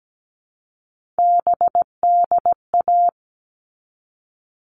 Morse code, heard as a keyed tone, about 17 wpm, 700 Hz. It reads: BDA
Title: All Cut Numbers - 3-Digit 17wpm